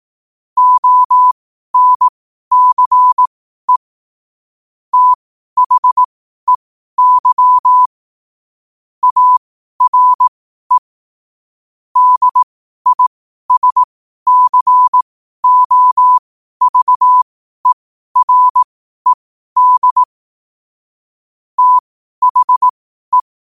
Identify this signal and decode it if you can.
Morse code, heard as a keyed tone, about 18 wpm, spaced as for 12 wpm, 1000 Hz. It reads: ONCE THEY ARE DISCOVERED THE
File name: qotdfiles / 12wpm